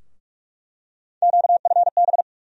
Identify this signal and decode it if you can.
Morse code, heard as a keyed tone, about 47 wpm, 700 Hz. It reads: QFB
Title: USA Call Sign Postfixes - 3 Letters 45wpm